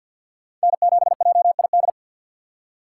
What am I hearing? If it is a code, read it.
Morse code, heard as a keyed tone, about 52 wpm, 700 Hz. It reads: N7JID
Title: USA Call Signs - 5 Characters - 4x Character Spacing - 50wpm